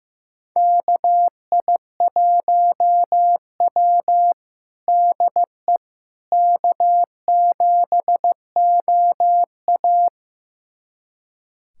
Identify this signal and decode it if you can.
Morse code, heard as a keyed tone, about 15 wpm, 700 Hz. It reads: KI1W DE K7OA
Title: USA Callsign DE Callsign - 4 Characters - 15wpm